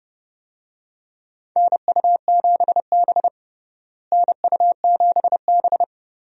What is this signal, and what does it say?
Morse code, heard as a keyed tone, about 30 wpm, 700 Hz. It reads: NU7B NU7B
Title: USA Call Signs - 4 Characters - Repeated 3x - 30wpm